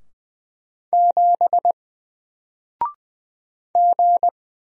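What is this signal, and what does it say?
Morse code, heard as a keyed tone, about 20 wpm, 700 Hz. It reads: 7 G
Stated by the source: Single letter and number Morse drill at 20 wpm, from N4LCD.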